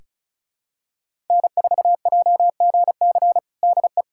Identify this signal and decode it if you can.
Morse code, heard as a keyed tone, about 36 wpm, 700 Hz. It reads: N4JGC DE
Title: USA Callsign DE Callsign 35wpm